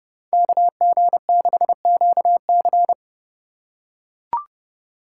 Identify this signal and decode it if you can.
Morse code, heard as a keyed tone, about 29 wpm, 700 Hz. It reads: KG6QC